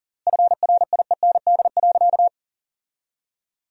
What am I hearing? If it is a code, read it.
Morse code, heard as a keyed tone, about 40 wpm, 700 Hz. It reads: FRIEND.
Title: 4x Word-Spacing - Sentences from Top 300 Words - 40wpm